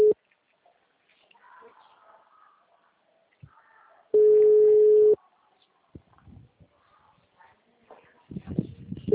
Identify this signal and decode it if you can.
Ringback tone (Europe, 425 Hz).
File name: out- -2012- - - .31192